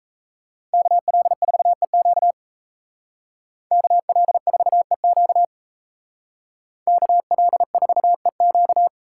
Morse code, heard as a keyed tone, about 33 wpm, 700 Hz. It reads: KL4EQ KL4EQ KL4EQ
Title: Speed Racing - USA 2x2 Call Signs 28wpm